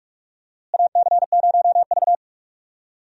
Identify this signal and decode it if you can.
Morse code, heard as a keyed tone, about 46 wpm, 700 Hz. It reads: AC0V
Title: USA Call Signs - 4 Characters - Repeated 3x - 45wpm